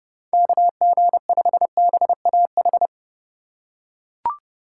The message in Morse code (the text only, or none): KG5BAH